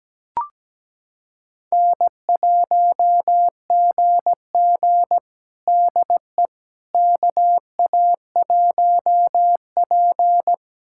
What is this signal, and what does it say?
Morse code, heard as a keyed tone, about 17 wpm, 700 Hz. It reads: N1GG DE KA1P
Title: USA Callsign DE Callsign - 4 Characters - 17wpm